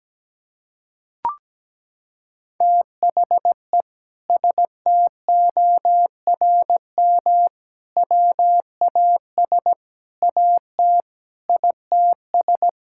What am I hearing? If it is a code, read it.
Morse code, heard as a keyed tone, about 17 wpm, 700 Hz. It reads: THE STORM WAS AT ITS
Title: Sentences from Top 1600 Words 17wpm